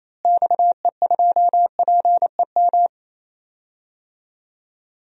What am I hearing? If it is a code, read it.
Morse code, heard as a keyed tone, about 28 wpm, 700 Hz. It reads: XE2PEM
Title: DXCC-100 Call Signs - 6 Characters - Repeated 3x - 28wpm